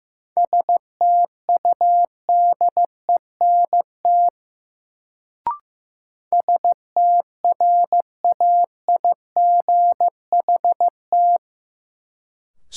Morse code, heard as a keyed tone, about 15 wpm, 700 Hz. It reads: STUDENT STRAIGHT
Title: Sentences from Top 1000 Words - New Word Drill - 15wpm